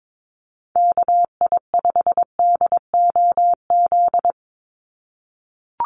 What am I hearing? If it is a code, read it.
Morse code, heard as a keyed tone, about 22 wpm, 700 Hz. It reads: KI5DOZ
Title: USA Call Signs 22wpm